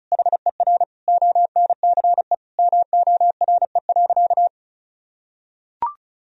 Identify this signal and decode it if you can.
Morse code, heard as a keyed tone, about 36 wpm, 700 Hz. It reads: HER ONCE MORE.
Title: Sentences from Top 500 Words and Abbreviations 35wpm